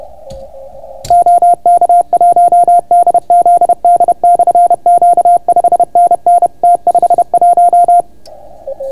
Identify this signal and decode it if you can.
Morse code, heard as a keyed tone, about 31 wpm, 700 Hz. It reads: OK1DZD/Q 5NN T51